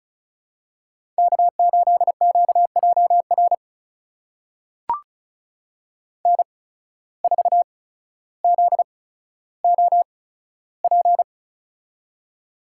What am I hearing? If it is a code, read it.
Morse code, heard as a keyed tone, about 36 wpm, spaced as for 15 wpm, 700 Hz. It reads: K8QJR N4ZOP
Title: USA Call Signs - 8x Character Spacing - 35wpm